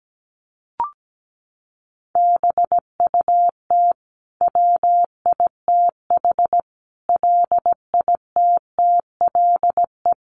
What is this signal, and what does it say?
Morse code, heard as a keyed tone, about 17 wpm, 700 Hz. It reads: BUT WITH LITTLE
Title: Sentences from Top 1400 Words 17wpm